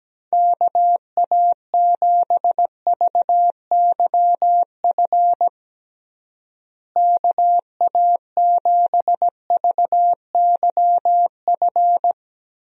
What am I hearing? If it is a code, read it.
Morse code, heard as a keyed tone, about 17 wpm, 700 Hz. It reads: KA7VYF KA7VYF
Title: USA Call Signs - 6 Characters - Repeated 3x - 17wpm